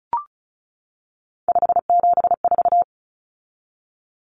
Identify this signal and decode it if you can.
Morse code, heard as a keyed tone, about 34 wpm, 700 Hz. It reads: 574
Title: Numbers - 3-Digit 35wpm